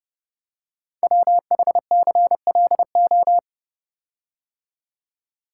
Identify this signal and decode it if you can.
Morse code, heard as a keyed tone, about 30 wpm, 700 Hz. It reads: WHCLO
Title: Code Groups - 5 Letter:Numbers - 30wpm